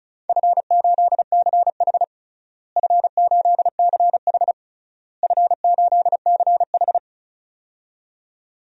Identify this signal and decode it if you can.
Morse code, heard as a keyed tone, about 36 wpm, 700 Hz. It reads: F8CH F8CH F8CH
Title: DXCC-100 Call Signs - 4 Characters - Repeated 3x - 35wpm